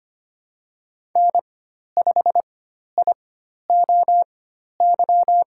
Morse code, heard as a keyed tone, about 25 wpm, spaced as for 12 wpm, 700 Hz. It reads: N5IOY
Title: USA Call Signs - 5 Characters - 4x Character Spacing - 25wpm